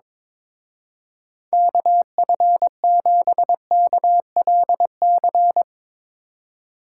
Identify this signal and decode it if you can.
Morse code, heard as a keyed tone, about 22 wpm, 700 Hz. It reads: KF7KLC
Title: Speed Racing - USA 2x3 Call Signs 22wpm